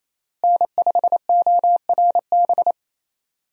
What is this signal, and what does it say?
Morse code, heard as a keyed tone, about 28 wpm, 700 Hz. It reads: N5ORB
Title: USA Call Signs - 5 Characters - Repeated 2x - 28wpm